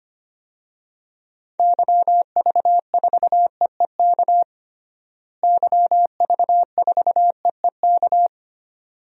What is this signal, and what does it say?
Morse code, heard as a keyed tone, about 25 wpm, 700 Hz. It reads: YV4EEK YV4EEK
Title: DXCC-100 Call Signs - 6 Characters - Repeated 2x - 25wpm